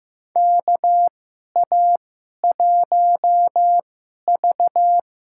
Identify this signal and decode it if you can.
Morse code, heard as a keyed tone, about 15 wpm, spaced as for 11 wpm, 700 Hz. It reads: KA1V
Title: USA Call Signs - 2x Character Spacing - 15wpm